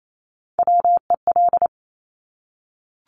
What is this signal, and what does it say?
Morse code, heard as a keyed tone, about 28 wpm, 700 Hz. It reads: WEL
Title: Top Three Letter Combinations in Words 28wpm